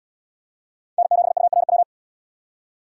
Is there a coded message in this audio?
Yes: Morse code from 1.0 to 1.8 s, heard as a keyed tone, about 77 wpm, 700 Hz. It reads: N7URW